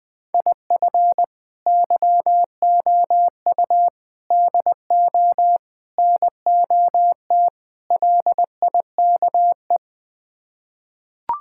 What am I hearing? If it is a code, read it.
Morse code, heard as a keyed tone, about 20 wpm, 700 Hz. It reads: IF YOU DO NOT LIKE
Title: Build A Phrase - 5-Word Phrases - 20wpm